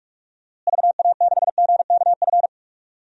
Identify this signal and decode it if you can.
Morse code, heard as a keyed tone, about 46 wpm, 700 Hz. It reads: VA6GKF
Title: DXCC-100 Call Signs - 6 Characters - Repeated 3x - 45wpm